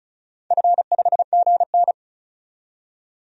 Morse code, heard as a keyed tone, about 36 wpm, 700 Hz. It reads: F5GN
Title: DXCC-100 Call Signs - 4 Characters - 35wpm